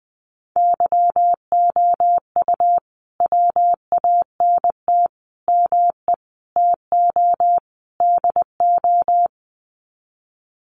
Morse code, heard as a keyed tone, about 20 wpm, 700 Hz. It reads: YOU WANT ME TO DO
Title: Sets of 5 Words Encore2 20wpm